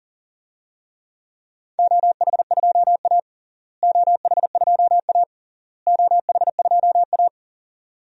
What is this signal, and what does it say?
Morse code, heard as a keyed tone, about 37 wpm, 700 Hz. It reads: OH2A OH2A OH2A
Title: DXCC-100 Call Signs - 4 Characters - Repeated 3x - 40wpm